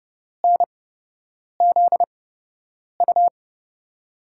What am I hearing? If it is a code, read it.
Morse code, heard as a keyed tone, about 30 wpm, spaced as for 8 wpm, 700 Hz. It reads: NZU
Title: USA Call Signs - 6 Characters - 8x Character Spacing - 30wpm